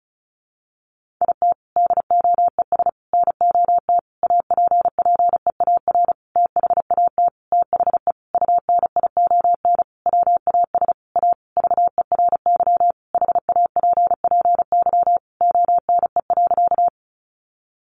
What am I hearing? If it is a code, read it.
Morse code, heard as a keyed tone, about 34 wpm, 700 Hz. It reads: IT DOES NOT APPEAR THAT THE UNION WAS A VERY HAPPY ONE.
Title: Sentences from Top 1400 Words 35wpm